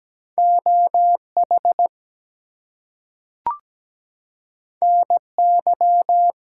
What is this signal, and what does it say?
Morse code, heard as a keyed tone, about 17 wpm, 700 Hz. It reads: OH NY
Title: US States and Canadian Provinces 17wpm